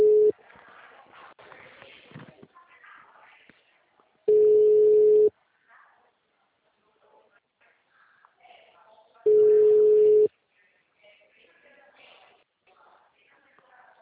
Ringback tone (Europe, 425 Hz).